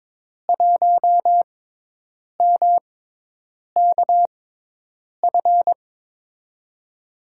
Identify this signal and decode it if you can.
Morse code, heard as a keyed tone, about 23 wpm, spaced as for 8 wpm, 700 Hz. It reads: 1MKF
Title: USA Call Signs - 5 Characters - 6x Character Spacing - 22wpm